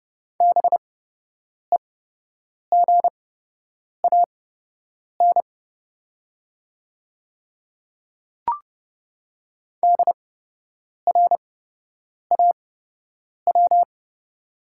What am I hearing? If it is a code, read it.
Morse code, heard as a keyed tone, about 30 wpm, spaced as for 8 wpm, 700 Hz. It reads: BEGAN DRAW
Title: 8x Character Spacing - Top 300 Words - 30wpm